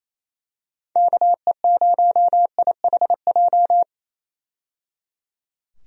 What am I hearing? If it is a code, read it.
Morse code, heard as a keyed tone, about 28 wpm, 700 Hz. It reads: KE0IHJ
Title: USA Call Signs - 6 Characters - 28wpm